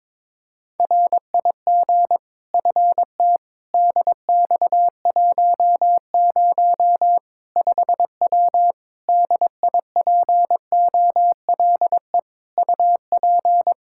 Morse code, heard as a keyed tone, about 21 wpm, 700 Hz. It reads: RIG FTDX10 5W DIPOLE UP
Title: Practice QSO - Exchange 2 - Info - 22wpm